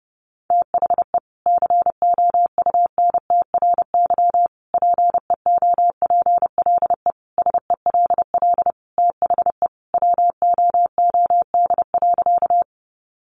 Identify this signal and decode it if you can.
Morse code, heard as a keyed tone, about 30 wpm, 700 Hz. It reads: THE COUNTRY PEOPLE SELL THE WOOD.
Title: Sentences from Top 1100 Words 30wpm